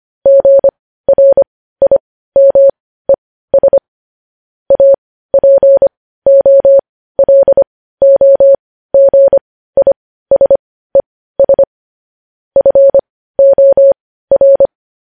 Morse code, heard as a keyed tone, about 25 wpm, spaced as for 15 wpm, 550 Hz. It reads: GRIMES APOLOGISES FOR
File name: podcast / News headlines-1x-25x15